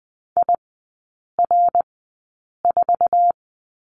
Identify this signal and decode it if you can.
Morse code, heard as a keyed tone, about 20 wpm, spaced as for 8 wpm, 700 Hz. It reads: IR4
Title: 067 Lesson 12e - Letter d - Three Characters 20wpm